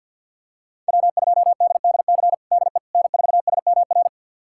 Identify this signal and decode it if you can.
Morse code, heard as a keyed tone, about 51 wpm, 700 Hz. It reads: W2DDC DE N4SMR
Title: USA Callsign DE Callsign - 5 Characters - 50wpm